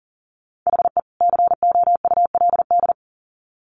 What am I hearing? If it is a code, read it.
Morse code, heard as a keyed tone, about 40 wpm, 700 Hz. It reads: HE COULD
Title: Sets of 2 Words 40wpm